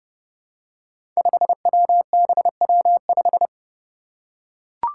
Morse code, heard as a keyed tone, about 28 wpm, 700 Hz. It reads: 5WBW5
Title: Five Challenging Letter-Number Combinations 30wpm